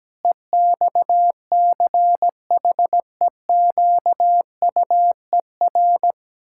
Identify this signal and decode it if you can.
Morse code, heard as a keyed tone, about 17 wpm, 700 Hz. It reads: EXCHEQUER
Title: Pangram Challenge v2 - 17wpm